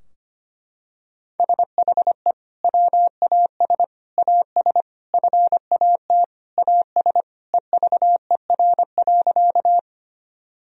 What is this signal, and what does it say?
Morse code, heard as a keyed tone, about 25 wpm, 700 Hz. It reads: SHE WAS AS FAT AS EVER.
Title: Sentences from Top 1000 Words - Review Part 4 - 25wpm